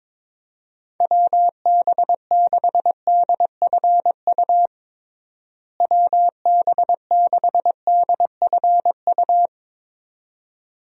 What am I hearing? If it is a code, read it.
Morse code, heard as a keyed tone, about 22 wpm, 700 Hz. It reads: WB6DFU WB6DFU
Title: USA Call Signs - 6 Characters - Repeated 2x - 22wpm